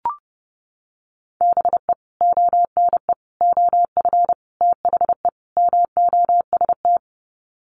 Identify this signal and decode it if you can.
Morse code, heard as a keyed tone, about 30 wpm, 700 Hz. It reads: BE ONE OF THE MOST
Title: Sets of 5 Words Encore 30wpm